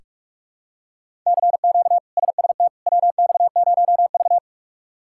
Morse code, heard as a keyed tone, about 46 wpm, 700 Hz. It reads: CQ SST WX0V
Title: K1USN SST QSO Practice - 45wpm